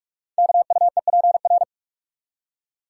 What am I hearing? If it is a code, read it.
Morse code, heard as a keyed tone, about 44 wpm, 700 Hz. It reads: KUEPR
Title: Code Groups - 5 Letters - 45wpm